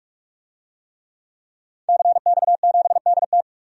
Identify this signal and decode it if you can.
Morse code, heard as a keyed tone, about 46 wpm, 700 Hz. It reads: CX7DT
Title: DXCC-100 Call Signs - 5 Characters - 45wpm